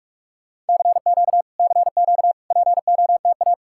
Morse code, heard as a keyed tone, about 59 wpm, 700 Hz. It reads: CQ CQ POTA